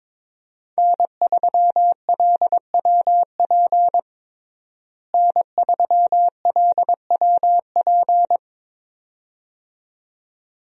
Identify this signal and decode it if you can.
Morse code, heard as a keyed tone, about 22 wpm, 700 Hz. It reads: N3LWP N3LWP
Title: USA Call Signs - 5 Characters - Repeated 2x - 22wpm